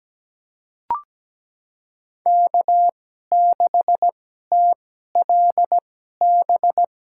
Morse code, heard as a keyed tone, about 17 wpm, spaced as for 12 wpm, 700 Hz. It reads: K6TLB
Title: USA Call Signs - 5 Characters - 2x Character Spacing - 17wpm